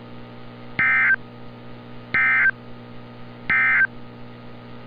Emergency Alert System (SAME) tones, heard as harsh buzzing bursts. It is heard from 0.7 to 3.9 s.
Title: NOAA Weather Radio - Altoona, PA